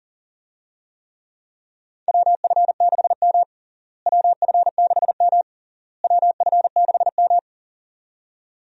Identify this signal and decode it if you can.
Morse code, heard as a keyed tone, about 40 wpm, 700 Hz. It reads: WF6M WF6M WF6M
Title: USA Call Signs - 4 Characters - Repeated 3x - 40wpm